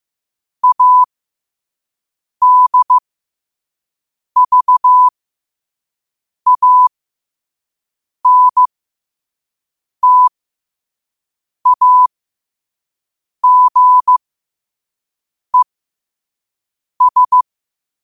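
Morse code, heard as a keyed tone, about 15 wpm, spaced as for 5 wpm, 1000 Hz. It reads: ADVANTAGES
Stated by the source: Morse code sent at 5 wpm.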